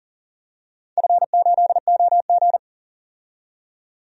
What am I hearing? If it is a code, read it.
Morse code, heard as a keyed tone, about 40 wpm, 700 Hz. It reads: F8OG
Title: DXCC-100 Call Signs - 4 Characters - 40wpm